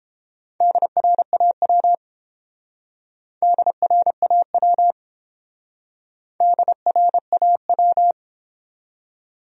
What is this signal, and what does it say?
Morse code, heard as a keyed tone, about 30 wpm, 700 Hz. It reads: DRAW DRAW DRAW